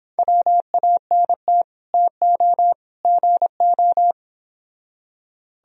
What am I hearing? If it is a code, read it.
Morse code, heard as a keyed tone, about 26 wpm, 700 Hz. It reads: WANT TO GO